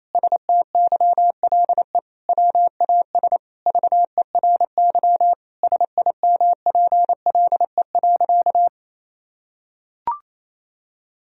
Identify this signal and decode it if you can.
Morse code, heard as a keyed tone, about 28 wpm, 700 Hz. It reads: STYLE WAS VERY SIMPLE.
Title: Sentences from Top 1000 Words - Review Part 4 - 28wpm